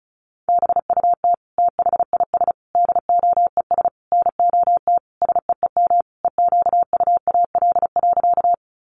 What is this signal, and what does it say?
Morse code, heard as a keyed tone, about 34 wpm, 700 Hz. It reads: BUT THIS DOES NOT SEEM EQUAL.
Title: Sentences from Top 1400 Words 35wpm